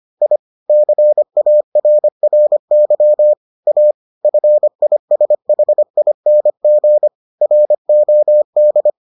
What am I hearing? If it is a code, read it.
Morse code, heard as a keyed tone, about 25 wpm, 600 Hz. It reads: I CARRY A FISHING ROD